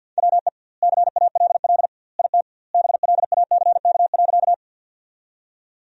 Morse code, heard as a keyed tone, about 47 wpm, 700 Hz. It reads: WE CALL IT BLACK.
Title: 2x Word-Spacing - Sentences from Top 400 Words - 50wpm